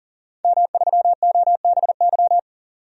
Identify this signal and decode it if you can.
Morse code, heard as a keyed tone, about 40 wpm, 700 Hz. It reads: M3OBY
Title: DXCC-100 Call Signs - 5 Characters - 40wpm